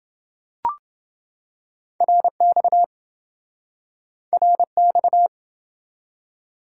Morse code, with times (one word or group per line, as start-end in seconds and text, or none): 2.0-2.8 RX
4.3-5.3 RX